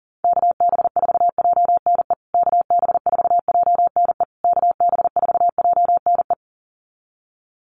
Morse code, heard as a keyed tone, about 37 wpm, 700 Hz. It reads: KB4JNE KB4JNE KB4JNE
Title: USA Call Signs Repeated 3x 40wpm